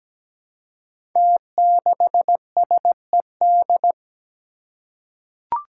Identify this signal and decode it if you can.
Morse code, heard as a keyed tone, about 17 wpm, 700 Hz. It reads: T6SED
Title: Five Challenging Letter-Number Combinations 17wpm